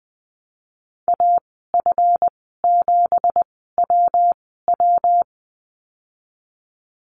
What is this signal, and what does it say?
Morse code, heard as a keyed tone, about 20 wpm, spaced as for 14 wpm, 700 Hz. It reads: AF7WW
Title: 200 Lesson 32g - Letter k - Call Signs 20wpm